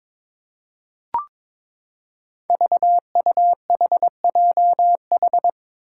Morse code, heard as a keyed tone, about 21 wpm, 700 Hz. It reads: VUHJH
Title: Five Challenging Letter-Number Combinations 22wpm